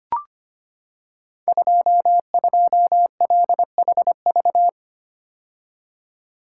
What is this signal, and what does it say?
Morse code, heard as a keyed tone, about 25 wpm, 700 Hz. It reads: 22LHV
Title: Code Groups - 5 Letter:Numbers - 25wpm